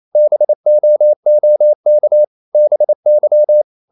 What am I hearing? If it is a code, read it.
Morse code, heard as a keyed tone, about 28 wpm, 600 Hz. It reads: BOOK BY
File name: CW / Practice / EnglishSayings28